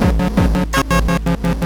There is no code.